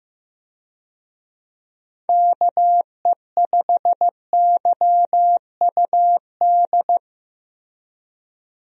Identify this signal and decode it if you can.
Morse code, heard as a keyed tone, about 15 wpm, 700 Hz. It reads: KE5YUD
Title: USA Call Signs - 6 Characters - Repeated 2x - 15wpm